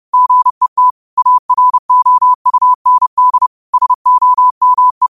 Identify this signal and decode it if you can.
Morse code, heard as a keyed tone, about 30 wpm, 1000 Hz. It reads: GET AROUND SOME
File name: qotdfiles / 30wpm